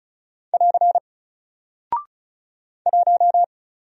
Morse code, heard as a keyed tone, about 35 wpm, 700 Hz. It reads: <AR> 1
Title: Single Character (Letter:Number:Punctuation:Prosign) 35wpm